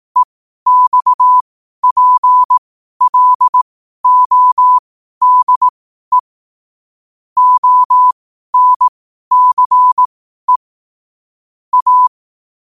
Morse code, heard as a keyed tone, about 18 wpm, spaced as for 12 wpm, 1000 Hz. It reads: EXPLODE ONCE A
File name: qotdfiles / 12wpm